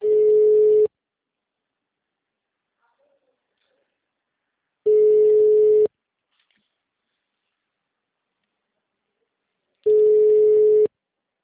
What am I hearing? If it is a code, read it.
Ringback tone (Europe, 425 Hz).